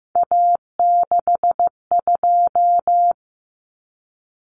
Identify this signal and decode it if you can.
Morse code, heard as a keyed tone, about 15 wpm, 700 Hz. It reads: A62
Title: Common Cut Numbers - 3-Digit 15wpm